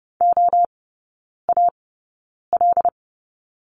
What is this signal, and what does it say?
Morse code, heard as a keyed tone, about 30 wpm, spaced as for 9 wpm, 700 Hz. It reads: OAL
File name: Lesson 19e - Letter w - Three Characters 30wpm